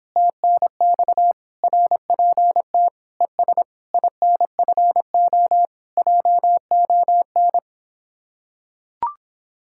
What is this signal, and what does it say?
Morse code, heard as a keyed tone, about 26 wpm, 700 Hz. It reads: TNX RPT ES INFO JON